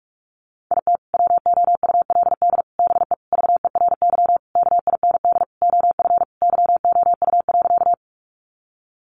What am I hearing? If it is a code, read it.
Morse code, heard as a keyed tone, about 44 wpm, 700 Hz. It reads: IT WOULD BE VERY KIND OF YOU.
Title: Sentences from Top 200 Words 45wpm